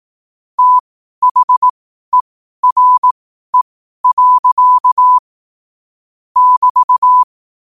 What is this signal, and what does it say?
Morse code, heard as a keyed tone, about 18 wpm, spaced as for 12 wpm, 1000 Hz. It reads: THERE. <BT>
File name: qotdfiles / 12wpm